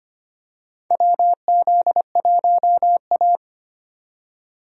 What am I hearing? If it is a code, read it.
Morse code, heard as a keyed tone, about 25 wpm, 700 Hz. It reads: WZ1A
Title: USA Call Signs - 4 Characters - 25wpm